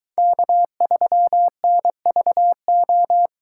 Morse code, heard as a keyed tone, about 23 wpm, 700 Hz. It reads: K3NVO